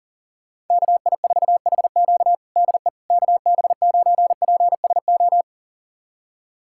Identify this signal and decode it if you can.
Morse code, heard as a keyed tone, about 40 wpm, 700 Hz. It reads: KI4HQ DE KB9PSO
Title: USA Callsign DE Callsign 40wpm